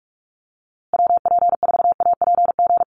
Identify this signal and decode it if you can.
Morse code, heard as a keyed tone, about 44 wpm, 700 Hz. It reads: WP4APG
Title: USA Call Signs Repeated 3x 45wpm